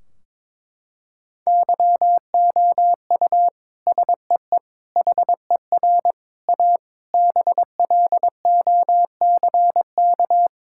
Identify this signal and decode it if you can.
Morse code, heard as a keyed tone, about 22 wpm, 700 Hz. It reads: YOU SEE HER A BLOCK
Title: Sentences from Top 1600 Words 22wpm